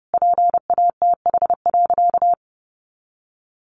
Morse code, heard as a keyed tone, about 30 wpm, 700 Hz. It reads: PATH.
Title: Sentences from Top 1400 Words 30wpm